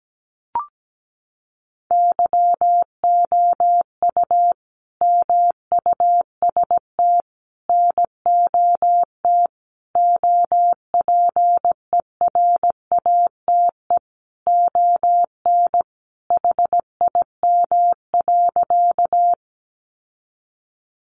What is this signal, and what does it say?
Morse code, heard as a keyed tone, about 17 wpm, 700 Hz. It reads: YOU MUST NOT OPERATE ON HIM.
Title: Sentences from Top 1500 Words 17wpm